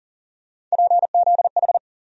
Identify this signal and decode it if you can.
Morse code, heard as a keyed tone, about 40 wpm, 700 Hz. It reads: PZH
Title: USA Call Sign Postfixes - 3 Letters 40wpm